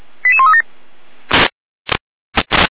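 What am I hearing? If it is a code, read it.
Five-tone selective call (ZVEI), heard as quick runs of five tones, tones of about 70 ms. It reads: 88137
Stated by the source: ZVEI five-tone call received via SDR.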